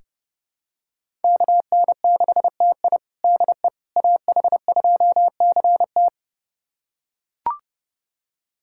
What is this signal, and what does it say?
Morse code, heard as a keyed tone, about 32 wpm, 700 Hz. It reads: KN6TI DE AH2CT
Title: USA Callsign DE Callsign - 5 Characters - 30wpm